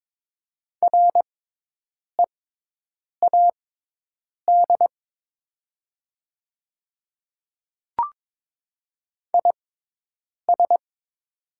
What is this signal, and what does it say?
Morse code, heard as a keyed tone, about 22 wpm, spaced as for 8 wpm, 700 Hz. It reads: READ IS
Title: 6x Character Spacing - Top 300 Words - 22wpm